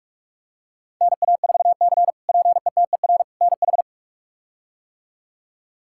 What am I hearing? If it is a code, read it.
Morse code, heard as a keyed tone, about 46 wpm, 700 Hz. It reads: NA4C PETER NH
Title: K1USN SST QSO Practice - 45wpm